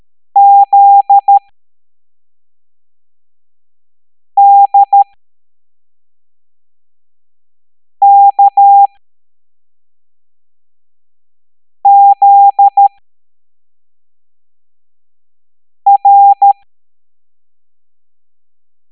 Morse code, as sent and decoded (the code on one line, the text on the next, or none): --.. -.. -.- --.. .-.
ZDKZR